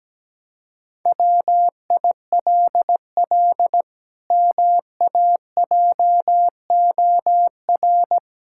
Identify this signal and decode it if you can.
Morse code, heard as a keyed tone, about 17 wpm, 700 Hz. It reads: WILL MAJOR
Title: Pangram Challenge v2 - 17wpm